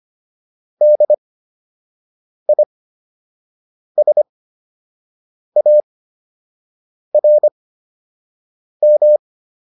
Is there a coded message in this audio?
Yes: Morse code from 0.8 to 9.2 s, heard as a keyed tone, about 25 wpm, spaced as for 6 wpm, 600 Hz. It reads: DISARM